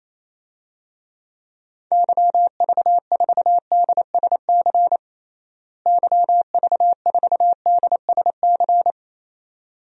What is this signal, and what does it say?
Morse code, heard as a keyed tone, about 27 wpm, 700 Hz. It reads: YV4DSC YV4DSC
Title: DXCC-100 Call Signs - 6 Characters - Repeated 2x - 28wpm